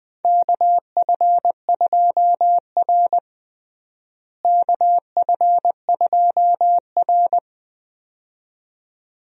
Morse code, heard as a keyed tone, about 21 wpm, 700 Hz. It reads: KF2R KF2R
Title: USA Call Signs - 4 Characters - Repeated 2x - 20wpm